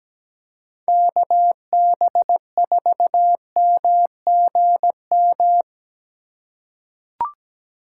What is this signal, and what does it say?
Morse code, heard as a keyed tone, about 17 wpm, 700 Hz. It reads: KB4MGM
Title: USA Call Signs - 6 Characters - 3x Character Spacing - 17wpm